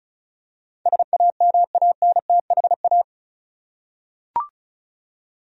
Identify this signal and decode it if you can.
Morse code, heard as a keyed tone, about 36 wpm, 700 Hz. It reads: SAMANTHA
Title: USA Common Names - 35wpm